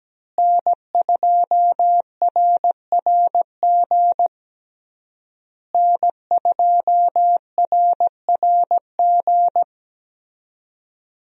Morse code, heard as a keyed tone, about 17 wpm, 700 Hz. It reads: N2RRG N2RRG
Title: USA Call Signs - 5 Characters - Repeated 3x - 17wpm